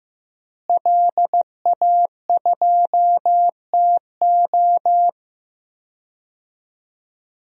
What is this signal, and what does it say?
Morse code, heard as a keyed tone, about 15 wpm, 700 Hz. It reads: LA2TO
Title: DXCC-100 Call Signs - 5 Characters - Repeated 2x - 15wpm